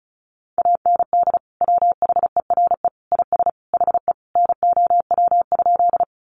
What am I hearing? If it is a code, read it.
Morse code, heard as a keyed tone, about 34 wpm, 700 Hz. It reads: AND WHERE IS HE NOW?
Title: Sentences from Top 200 Words 35wpm